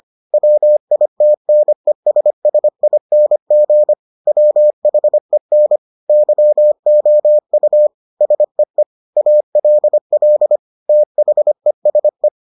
Morse code, heard as a keyed tone, about 25 wpm, 600 Hz. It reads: WITNESSING WHEN YOU SEE ALL THESE